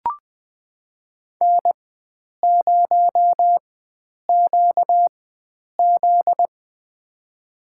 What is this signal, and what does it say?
Morse code, heard as a keyed tone, about 20 wpm, spaced as for 9 wpm, 700 Hz. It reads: N0QZ
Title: USA Call Signs - 4 Characters - 4x Character Spacing - 20wpm